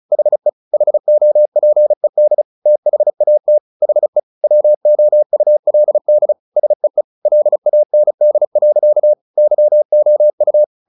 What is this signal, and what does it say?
Morse code, heard as a keyed tone, about 34 wpm, 600 Hz. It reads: HE HOPED THAT HE WOULD SEE LAND. YOU
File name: the old man and the sea 2 35wpm